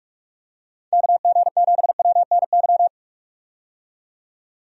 Morse code, heard as a keyed tone, about 45 wpm, 700 Hz. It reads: KG7WNY
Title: USA Call Signs - 6 Characters - 45wpm